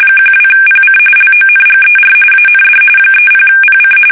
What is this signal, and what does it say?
RTTY, about 45.45 baud. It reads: 12PC LONGER AND 30PC
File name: fsk850 / battleshipnj